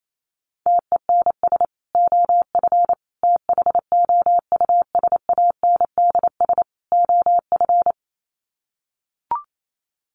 Morse code, heard as a keyed tone, about 28 wpm, 700 Hz. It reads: TENS OF THOUSANDS OF
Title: Sets of 4 Words 28wpm